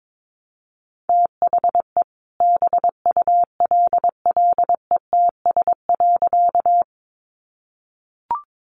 Morse code, heard as a keyed tone, about 22 wpm, 700 Hz. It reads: THE BULLETS.
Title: A Sound Of Thunder 22wpm